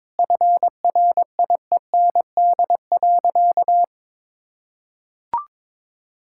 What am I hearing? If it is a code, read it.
Morse code, heard as a keyed tone, about 23 wpm, 700 Hz. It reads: FRIEND.
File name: practice / 22wpm / Sentences with 3-Word Phrases - 22wpm